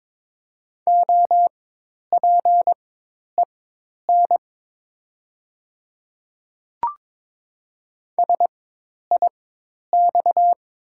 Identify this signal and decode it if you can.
Morse code, heard as a keyed tone, about 21 wpm, spaced as for 10 wpm, 700 Hz. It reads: OPEN SIX
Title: 4x Character Spacing - Top 500 Words - 22wpm